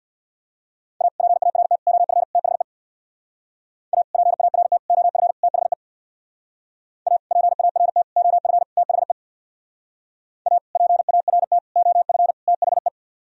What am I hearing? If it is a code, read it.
Morse code, heard as a keyed tone, about 62 wpm, 700 Hz. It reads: A PART OF THE A PART OF THE A PART OF THE A PART OF THE
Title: Speed-Racing - Sets of 4 Words 50wpm